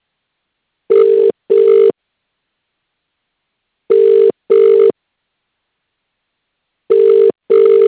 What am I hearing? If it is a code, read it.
Ringback tone (UK), 400 + 450 Hz.